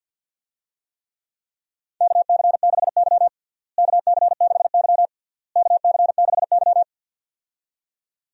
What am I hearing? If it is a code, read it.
Morse code, heard as a keyed tone, about 52 wpm, 700 Hz. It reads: KC6Y KC6Y KC6Y
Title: USA Call Signs - 4 Characters - Repeated 3x - 50wpm